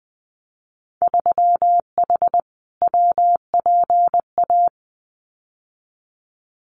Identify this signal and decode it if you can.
Morse code, heard as a keyed tone, about 19 wpm, 700 Hz. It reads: 3H WPA
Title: Winter Field Day Exchanges Encore 20wpm